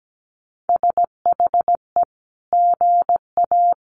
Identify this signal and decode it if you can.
Morse code, heard as a keyed tone, about 17 wpm, 700 Hz. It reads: SHE GA
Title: Sentences from Top 1400 Words 17wpm